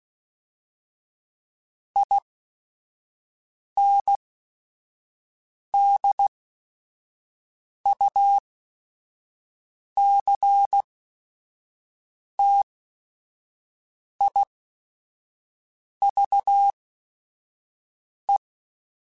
Morse code, heard as a keyed tone, about 16 wpm, spaced as for 5 wpm, 800 Hz. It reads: INDUCTIVE
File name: Technical Subjects / Technical Subjects-mp3s / XL @5